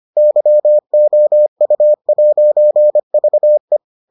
Morse code, heard as a keyed tone, about 25 wpm, 600 Hz. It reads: YOU'VE